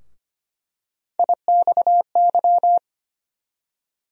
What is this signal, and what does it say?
Morse code, heard as a keyed tone, about 25 wpm, 700 Hz. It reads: IXY
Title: USA Call Sign Postfixes - 3 Letters 25wpm